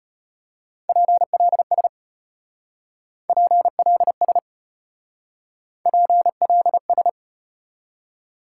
Morse code, heard as a keyed tone, about 32 wpm, 700 Hz. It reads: PLS PLS PLS